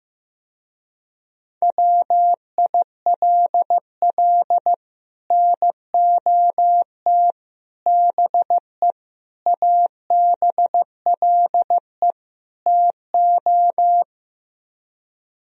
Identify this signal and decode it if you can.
Morse code, heard as a keyed tone, about 15 wpm, 700 Hz. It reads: WILL NOT BE ABLE TO
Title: Speed-Racing - Sets of 5 Words 15wpm